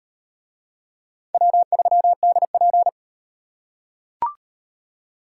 Morse code, heard as a keyed tone, about 38 wpm, 700 Hz. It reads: W3DP